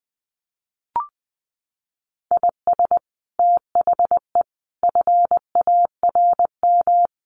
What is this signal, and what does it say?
Morse code, heard as a keyed tone, about 20 wpm, 700 Hz. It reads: IS THE FARM
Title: Sentences from Top 1400 Words 20wpm